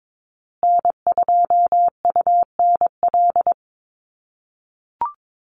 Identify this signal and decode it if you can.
Morse code, heard as a keyed tone, about 22 wpm, 700 Hz. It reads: N2UNL
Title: USA Call Signs Repeated 2x 22wpm